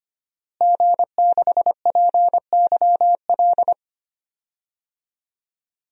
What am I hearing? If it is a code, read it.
Morse code, heard as a keyed tone, about 25 wpm, 700 Hz. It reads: G6PYL
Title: DXCC-100 Call Signs - 5 Characters - Repeated 2x - 25wpm